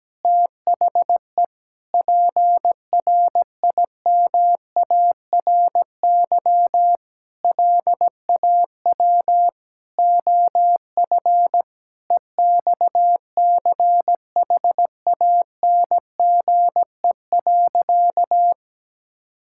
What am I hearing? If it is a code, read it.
Morse code, heard as a keyed tone, about 17 wpm, 700 Hz. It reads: THE PRIMARY LAW OF EXCHANGE.
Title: Sentences from Top 1000 Words - Review Part 6 - 17wpm